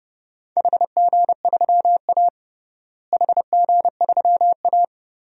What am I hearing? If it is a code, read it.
Morse code, heard as a keyed tone, about 30 wpm, 700 Hz. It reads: HG3A HG3A
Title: DXCC-100 Call Signs - 4 Characters - Repeated 3x - 30wpm